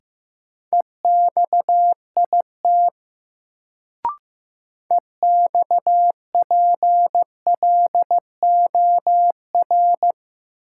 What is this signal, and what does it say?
Morse code, heard as a keyed tone, about 15 wpm, 700 Hz. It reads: EXIT EXPLOR
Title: Sentences from Top 1100 Words - New Word Drill - 15wpm